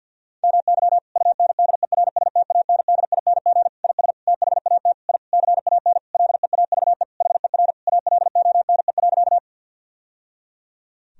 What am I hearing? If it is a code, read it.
Morse code, heard as a keyed tone, about 48 wpm, 700 Hz. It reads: MY UNDERSTANDING IS THAT I CAN LEAVE HER ALONE.
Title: Sentences from Top 1000 Words - Review Part 3 - 50wpm